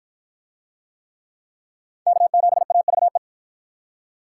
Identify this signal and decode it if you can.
Morse code, heard as a keyed tone, about 53 wpm, 700 Hz. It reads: K7AVE